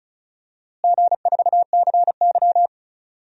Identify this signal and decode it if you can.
Morse code, heard as a keyed tone, about 36 wpm, 700 Hz. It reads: G4CY
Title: DXCC-100 Call Signs - 4 Characters - Repeated 2x - 35wpm